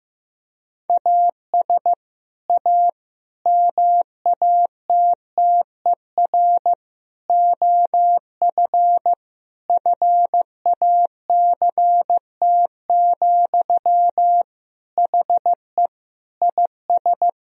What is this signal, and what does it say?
Morse code, heard as a keyed tone, about 15 wpm, 700 Hz. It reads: AS A MATTER OF FACT, HE IS
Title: Sentences from Top 1000 Words - Review Part 3 - 15wpm